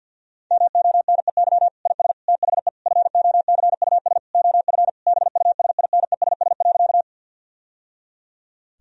Morse code, heard as a keyed tone, about 50 wpm, 700 Hz. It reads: MONEY IS THE FOCUS OF BUSINESS.
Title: Sentences from Top 1000 Words - Review Part 4 - 50wpm